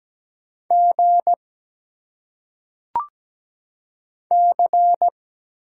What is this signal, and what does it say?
Morse code, heard as a keyed tone, about 17 wpm, 690 Hz. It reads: G C
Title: Single Letters - Phonetic Pronunciation 17wpm